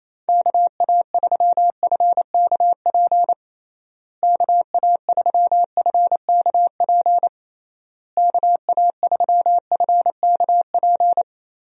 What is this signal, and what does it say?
Morse code, heard as a keyed tone, about 28 wpm, 700 Hz. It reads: KA3FKP KA3FKP KA3FKP
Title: USA Call Signs - 6 Characters - Repeated 3x - 28wpm